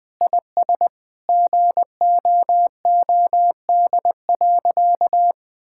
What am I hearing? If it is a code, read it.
Morse code, heard as a keyed tone, about 21 wpm, 700 Hz. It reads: IS GOOD.
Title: Sentences with 4-Word Phrases - Spoken Beginning - 20wpm